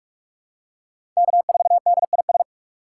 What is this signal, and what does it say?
Morse code, heard as a keyed tone, about 48 wpm, 700 Hz. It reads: K4DIS
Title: USA Call Signs - 5 Characters - 6x Character Spacing - 45wpm